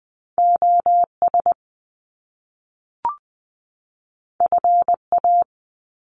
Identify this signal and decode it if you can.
Morse code, heard as a keyed tone, about 20 wpm, 700 Hz. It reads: OS FA
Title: Top Two Letter Combinations in Words 20wpm